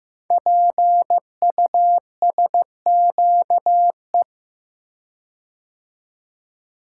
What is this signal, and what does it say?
Morse code, heard as a keyed tone, about 15 wpm, 700 Hz. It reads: PUSQE
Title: Code Groups - 5 Letters - 15wpm